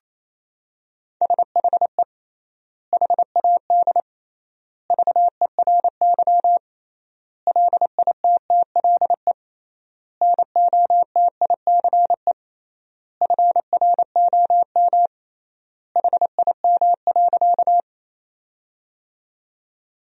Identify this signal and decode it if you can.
Morse code, heard as a keyed tone, about 28 wpm, 700 Hz. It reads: SHE HAD VERY LITTLE NOTICE FROM HIM.
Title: 3x Word-Spacing - Sentences from Top 400 Words - 28wpm